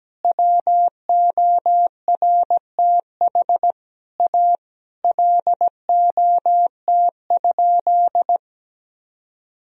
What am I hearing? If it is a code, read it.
Morse code, heard as a keyed tone, about 17 wpm, 700 Hz. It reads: WORTH A LOT?
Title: Sentences from Top 1000 Words - Review Part 4 - 17wpm